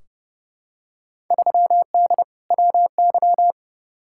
Morse code, heard as a keyed tone, about 29 wpm, 700 Hz. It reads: 3D WY
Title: Field Day Exchanges Encore - 30wpm